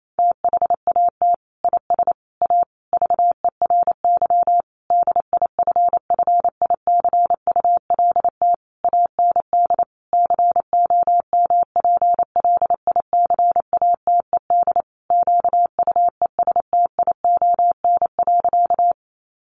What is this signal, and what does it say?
Morse code, heard as a keyed tone, about 28 wpm, 700 Hz. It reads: THAT IS A VERY DIFFICULT AND COMPLICATED QUESTION.
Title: Sentences from Top 1200 Words 28wpm